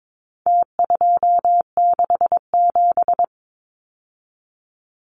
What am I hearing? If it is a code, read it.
Morse code, heard as a keyed tone, about 22 wpm, 700 Hz. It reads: T267
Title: Numbers - Contest Style - 4-Digit 22wpm